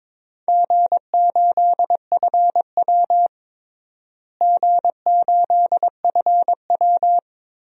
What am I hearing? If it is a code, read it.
Morse code, heard as a keyed tone, about 22 wpm, 700 Hz. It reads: G8FW G8FW
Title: DXCC-100 Call Signs - 4 Characters - Repeated 3x - 22wpm